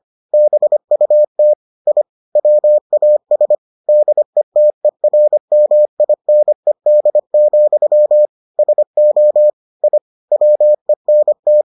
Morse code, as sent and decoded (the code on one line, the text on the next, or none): -... ..- - / .. / .-- .- ... / -.. - . .-. -- .. -. -.. --..-- / ... --- / .. / .-- . -. -
BUT I WAS DETERMINED, SO I WENT